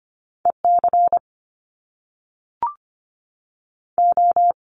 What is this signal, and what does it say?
Morse code, heard as a keyed tone, about 25 wpm, 700 Hz. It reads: E C O
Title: Top Two Letter Combinations in Words 25wpm